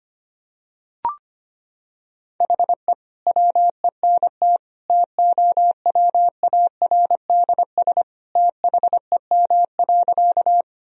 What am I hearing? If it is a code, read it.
Morse code, heard as a keyed tone, about 25 wpm, 700 Hz. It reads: HE WENT TOWARDS THEM.
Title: Sentences from Top 1000 Words - Review Part 2 - 25wpm